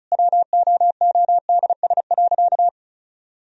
Morse code, heard as a keyed tone, about 36 wpm, 700 Hz. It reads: WOODS.
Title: Sentences from Top 1000 Words - Review Part 2 - 35wpm